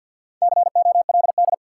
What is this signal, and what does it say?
Morse code, heard as a keyed tone, about 53 wpm, 700 Hz. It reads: COLD
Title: Sets of 5 Words - New Word Drill - 50wpm